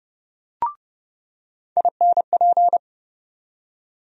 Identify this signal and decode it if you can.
Morse code, heard as a keyed tone, about 32 wpm, 700 Hz. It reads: INP
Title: USA Call Sign Postfixes - 3 Letters 30wpm